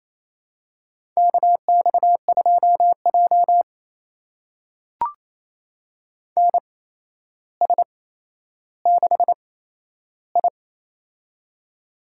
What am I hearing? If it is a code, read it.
Morse code, heard as a keyed tone, about 28 wpm, spaced as for 12 wpm, 700 Hz. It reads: KX2J NS6I
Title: USA Call Signs - 4 Characters - 8x Character Spacing - 28wpm